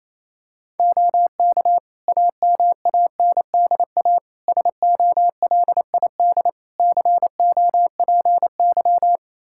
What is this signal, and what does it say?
Morse code, heard as a keyed tone, about 28 wpm, 700 Hz. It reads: OK AMANDA SOLID COPY
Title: K1USN SST QSO Practice 28wpm